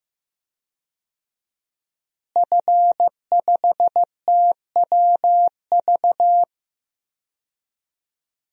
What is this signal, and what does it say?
Morse code, heard as a keyed tone, about 15 wpm, 700 Hz. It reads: F5TWV
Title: DXCC-100 Call Signs - 5 Characters - Repeated 3x - 15wpm